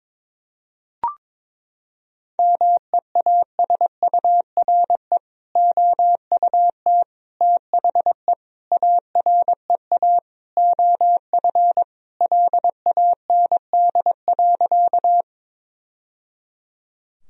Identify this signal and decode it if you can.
Morse code, heard as a keyed tone, about 22 wpm, 700 Hz. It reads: MEASURE OUT THE AREA OF LAND.
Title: Sentences from Top 1000 Words - Review Part 2 - 22wpm